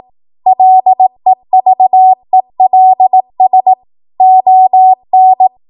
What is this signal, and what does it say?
Morse code, heard as a keyed tone, about 18 wpm, 750 Hz. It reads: LEVELS ON
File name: Morse / 18WPM